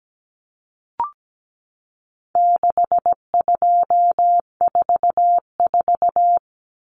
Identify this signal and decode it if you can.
Morse code, heard as a keyed tone, about 17 wpm, 700 Hz. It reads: 6244
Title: Numbers - 4-Digit 17wpm